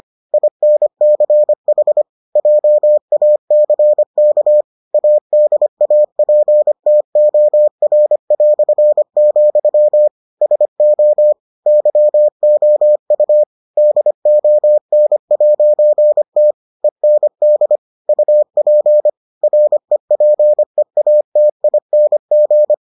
Morse code, heard as a keyed tone, about 25 wpm, 600 Hz. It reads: INCH JACK ADAPTOR", SO YOU DON'T END UP REPEATING